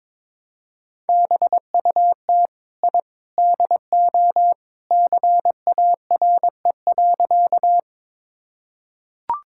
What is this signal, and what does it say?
Morse code, heard as a keyed tone, about 22 wpm, 700 Hz. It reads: BUT I DO CARE.
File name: practice / 22wpm / Sentences with 3-Word Phrases - 22wpm